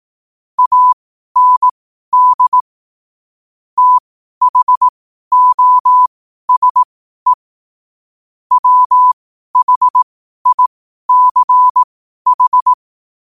Morse code, heard as a keyed tone, about 18 wpm, spaced as for 12 wpm, 1000 Hz. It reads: AND THOSE WHICH
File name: qotdfiles / 12wpm